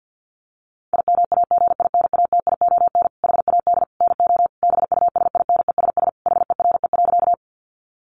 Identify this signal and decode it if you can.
Morse code, heard as a keyed tone, about 48 wpm, 700 Hz. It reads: IMAGINATION HAD NO BUSINESS HERE.
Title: Sentences from Top 1300 Words 50wpm